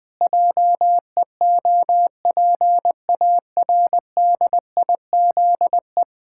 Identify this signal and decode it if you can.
Morse code, heard as a keyed tone, about 20 wpm, 700 Hz. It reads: JEOPARDIZE
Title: Pangram Challenge v2 - 20wpm